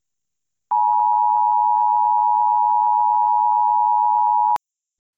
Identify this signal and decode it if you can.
Emergency Alert System (SAME) tones (the 853 and 960 Hz attention tone). It is heard from 0.7 to 4.6 s.
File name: Seek-Shelter-Now